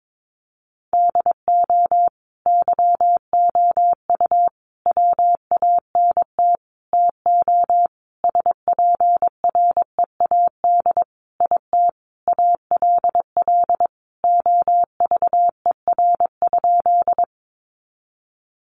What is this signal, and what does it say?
Morse code, heard as a keyed tone, about 22 wpm, 700 Hz. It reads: DO YOU WANT TO SPREAD IT ALL OVER?
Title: Sentences from Top 1200 Words 22wpm